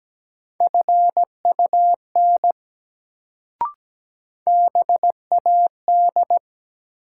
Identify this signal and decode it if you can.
Morse code, heard as a keyed tone, about 17 wpm, 700 Hz. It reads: FUN BAD
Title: Sentences from Top 700 Words - New Word Drill - 17wpm